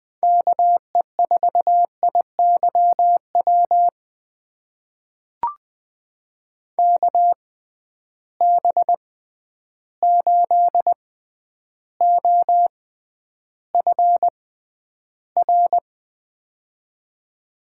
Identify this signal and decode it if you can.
Morse code, heard as a keyed tone, about 20 wpm, spaced as for 10 wpm, 700 Hz. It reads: KE4IYW KB8OFR
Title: USA Call Signs - 6 Characters - 6x Character Spacing - 20wpm